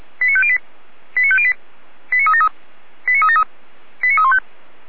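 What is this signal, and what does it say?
Five-tone selective call (ZVEI), heard as quick runs of five tones, tones of about 70 ms. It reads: 89508 89508 89373 89373 89315
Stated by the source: ZVEI five-tone call received by SDR, from feuerwehr-orlamuende.org (fire brigade Orlamünde) archive.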